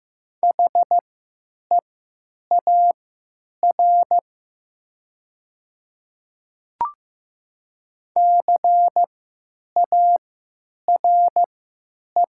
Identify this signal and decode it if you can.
Morse code, heard as a keyed tone, about 15 wpm, spaced as for 9 wpm, 700 Hz. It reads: HEAR CARE
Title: 3x Character Spacing - Top 300 Words - 15wpm